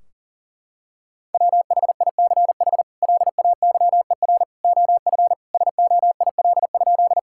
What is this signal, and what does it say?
Morse code, heard as a keyed tone, about 40 wpm, 700 Hz. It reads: WHICH LAYER OF SOIL?
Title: Sentences from Top 1600 Words 40wpm